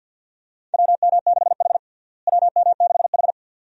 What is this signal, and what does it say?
Morse code, heard as a keyed tone, about 52 wpm, 700 Hz. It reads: WM6H WM6H
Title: USA Call Signs - 4 Characters - Repeated 2x - 50wpm